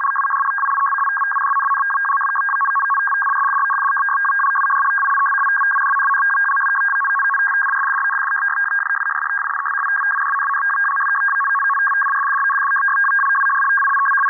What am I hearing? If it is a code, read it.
Two-tone teleprinter signal, about 500 Hz shift.